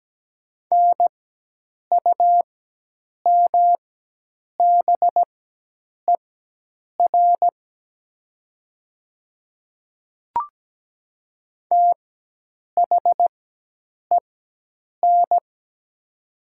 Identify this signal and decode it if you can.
Morse code, heard as a keyed tone, about 17 wpm, spaced as for 8 wpm, 700 Hz. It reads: NUMBER THEN
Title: 4x Character Spacing - Top 200 Words - 17wpm